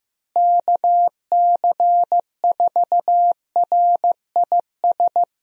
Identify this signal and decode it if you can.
Morse code, heard as a keyed tone, about 16 wpm, 700 Hz. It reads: KC4RIS
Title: USA Call Signs - 6 Characters - Repeated 3x - 15wpm